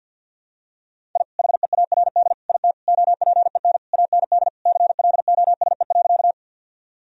Morse code, heard as a keyed tone, about 51 wpm, 700 Hz. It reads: I HEARD IT OPEN AND CLOSE.
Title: Sentences from Top 1000 Words - Review Part 1 - 50wpm